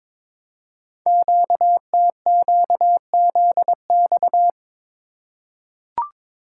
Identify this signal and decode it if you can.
Morse code, heard as a keyed tone, about 22 wpm, 700 Hz. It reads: QTQZX
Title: Code Groups - 5 Letters - 22wpm